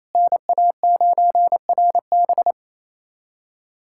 Morse code, heard as a keyed tone, about 27 wpm, 700 Hz. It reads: NA9RB
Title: USA Callsign DE Callsign - 5 Characters - 28wpm